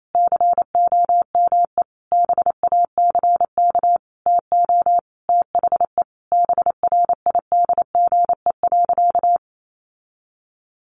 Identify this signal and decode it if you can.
Morse code, heard as a keyed tone, about 28 wpm, 700 Hz. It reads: COME BACK TO THE BRIDGE.
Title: Sentences from Top 1200 Words 28wpm